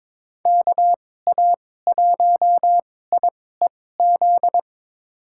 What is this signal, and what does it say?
Morse code, heard as a keyed tone, about 22 wpm, spaced as for 16 wpm, 700 Hz. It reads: KA1IEZ
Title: USA Call Signs - 2x Character Spacing - 22wpm